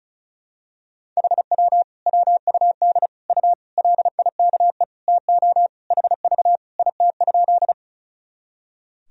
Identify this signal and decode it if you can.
Morse code, heard as a keyed tone, about 36 wpm, 700 Hz. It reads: HW WUD U LIKE TO HV IT?
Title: Sentences from Top 100 Words and Abbreviations 35wpm